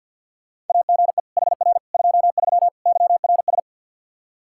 Morse code, heard as a keyed tone, about 45 wpm, 700 Hz. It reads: AGE HR 23 YRS
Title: K1USN SST QSO Practice 50wpm